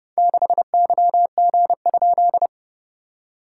Morse code, heard as a keyed tone, about 30 wpm, 690 Hz. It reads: 6YG?
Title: SOTA QSO Practice 30wpm